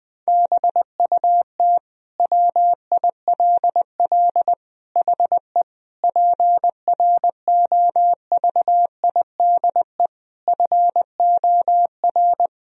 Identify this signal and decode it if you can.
Morse code, heard as a keyed tone, about 21 wpm, 700 Hz. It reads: BUT WILL HE PROVIDE FOR
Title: Sentences from Top 1000 Words - Review Part 3 - 20wpm